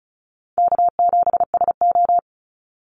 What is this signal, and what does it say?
Morse code, heard as a keyed tone, about 34 wpm, 700 Hz. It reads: K7SO
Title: NA SOTA Activator Callsigns 35wpm